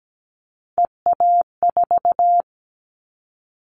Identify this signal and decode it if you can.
Morse code, heard as a keyed tone, about 17 wpm, 700 Hz. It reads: EA4
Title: Callsign Prefixes for Top 25 DXCC - Repeated 3x - 17wpm